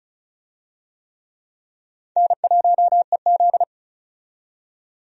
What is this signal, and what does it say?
Morse code, heard as a keyed tone, about 34 wpm, 700 Hz. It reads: N1EZ
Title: USA Call Signs - 4 Characters - 35wpm